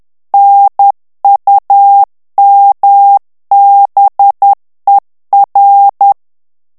Morse code, heard as a keyed tone, about 10 wpm, 800 Hz. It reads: NUMBER
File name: ANGLAIS texte clair 3 vitesse 10 mots minute